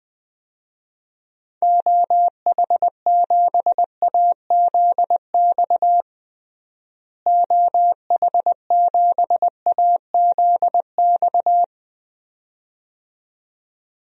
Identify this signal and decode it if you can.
Morse code, heard as a keyed tone, about 21 wpm, 700 Hz. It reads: OH7AZX OH7AZX
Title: DXCC-100 Call Signs - 6 Characters - Repeated 2x - 20wpm